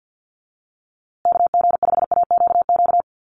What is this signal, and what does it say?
Morse code, heard as a keyed tone, about 50 wpm, 700 Hz. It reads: KG5AQQ
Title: USA Call Signs Repeated 3x 50wpm